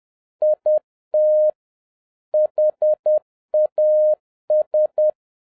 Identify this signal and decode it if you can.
Morse code, heard as a keyed tone, about 10 wpm, 610 Hz. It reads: IT HAS